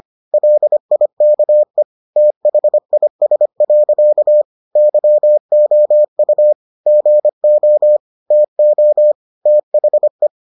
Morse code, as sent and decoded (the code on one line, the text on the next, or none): .-.. .. -.- . / - .... .. ... .-.-.- / -.-- --- ..- / --. --- / - --- / - .... .
LIKE THIS. YOU GO TO THE